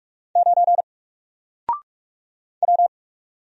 Morse code, heard as a keyed tone, about 46 wpm, 700 Hz. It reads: 9 W
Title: Single Letter-Number Alt 45wpm